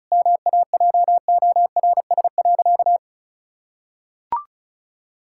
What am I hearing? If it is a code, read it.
Morse code, heard as a keyed tone, about 36 wpm, 700 Hz. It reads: MAJORS.
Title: Pangram Challenge v2 - 35wpm